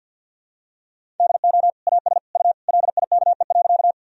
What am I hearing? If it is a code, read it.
Morse code, heard as a keyed tone, about 49 wpm, 700 Hz. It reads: DO AS U LIKE.
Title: Sentences from Top 100 Words and Abbreviations 50wpm